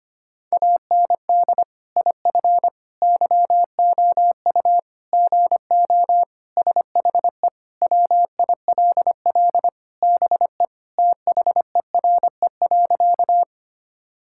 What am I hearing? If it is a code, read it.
Morse code, heard as a keyed tone, about 25 wpm, 700 Hz. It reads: AND IF YOU GO SHE WILL BE THERE.
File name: practice / 25wpm / Sentences with 3-Word Phrases - 25wpm